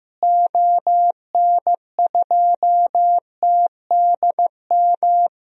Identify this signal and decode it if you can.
Morse code, heard as a keyed tone, about 16 wpm, 700 Hz. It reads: ON2TDM
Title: DXCC-100 Call Signs - 6 Characters - Repeated 2x - 15wpm